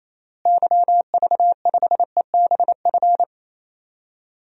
Morse code, heard as a keyed tone, about 28 wpm, 700 Hz. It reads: YV5EBF
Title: DXCC-100 Call Signs - 6 Characters - Repeated 2x - 28wpm